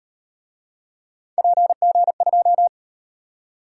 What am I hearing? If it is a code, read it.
Morse code, heard as a keyed tone, about 38 wpm, 700 Hz. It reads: PG2